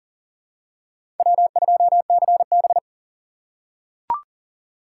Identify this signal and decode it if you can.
Morse code, heard as a keyed tone, about 40 wpm, 700 Hz. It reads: W2CB
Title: USA Call Signs - 4 Characters - 3x Character Spacing - 40wpm